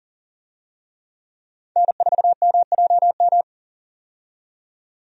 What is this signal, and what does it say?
Morse code, heard as a keyed tone, about 40 wpm, 700 Hz. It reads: N4MJM
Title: USA Call Signs - 5 Characters - 40wpm